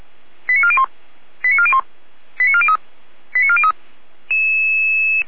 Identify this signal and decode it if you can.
Five-tone selective call (ZVEI), heard as quick runs of five tones, tones of about 70 ms. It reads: 89401 89401 89403 89403